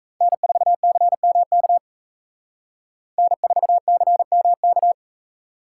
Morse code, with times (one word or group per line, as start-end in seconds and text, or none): 0.2-1.8 N4CMK
3.2-4.9 N4CMK